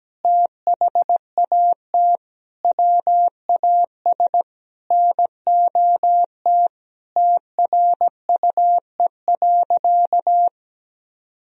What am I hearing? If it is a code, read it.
Morse code, heard as a keyed tone, about 17 wpm, 700 Hz. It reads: THAT WAS NOT TRUE.
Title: Sentences from Top 1000 Words - Review Part 4 - 17wpm